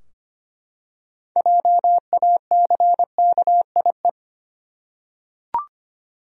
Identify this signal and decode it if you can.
Morse code, heard as a keyed tone, about 25 wpm, 700 Hz. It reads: JACKIE
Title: USA Common Names - 25wpm